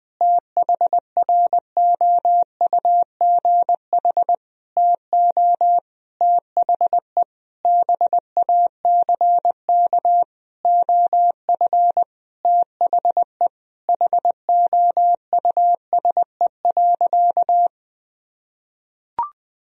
Morse code, heard as a keyed tone, about 20 wpm, 700 Hz. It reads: THROUGH TO THE BACK OF THE HOUSE.
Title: Sentences from Top 1000 Words - Review Part 1 - 20wpm